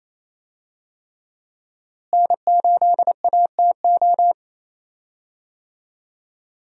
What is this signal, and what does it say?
Morse code, heard as a keyed tone, about 27 wpm, 700 Hz. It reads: N8ATO
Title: USA Call Signs - 5 Characters - Repeated 3x - 28wpm